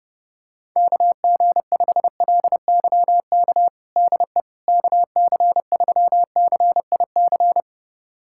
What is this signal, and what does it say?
Morse code, heard as a keyed tone, about 30 wpm, 700 Hz. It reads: KG5LYK DE KC3CIC
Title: USA Callsign DE Callsign - 6 Characters - 30wpm